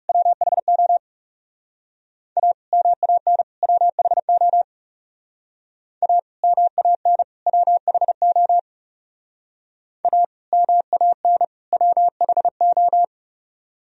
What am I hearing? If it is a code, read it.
Morse code, heard as a keyed tone, about 38 wpm, 700 Hz. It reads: WHO A MAN WHO A MAN WHO A MAN WHO